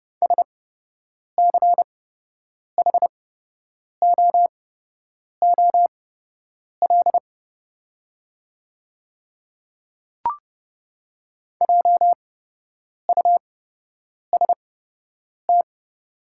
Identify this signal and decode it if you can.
Morse code, heard as a keyed tone, about 30 wpm, spaced as for 8 wpm, 700 Hz. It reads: SCHOOL JUST
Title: 8x Character Spacing - Top 300 Words - 30wpm